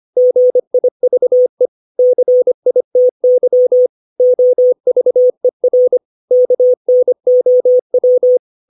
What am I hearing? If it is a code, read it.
Morse code, heard as a keyed tone, about 25 wpm, 500 Hz. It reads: GIVE CITY OVER KNOW
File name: 632 words / 632 Words 25WPM